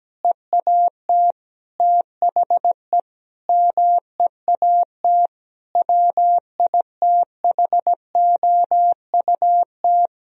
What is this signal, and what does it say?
Morse code, heard as a keyed tone, about 17 wpm, 700 Hz. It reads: EAT THE MEAT WITHOUT
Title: Sentences from Top 1000 Words - Review Part 3 - 17wpm